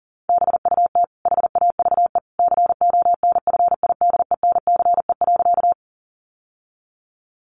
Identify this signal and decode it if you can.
Morse code, heard as a keyed tone, about 40 wpm, 700 Hz. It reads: BUT HAVE CONFIDENCE.
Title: Sentences from Top 1500 Words 40wpm